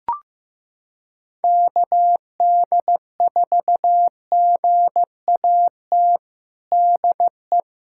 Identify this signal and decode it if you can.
Morse code, heard as a keyed tone, about 15 wpm, 700 Hz. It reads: KD4GAT DE
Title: USA Callsign DE Callsign - 6 Characters - 15wpm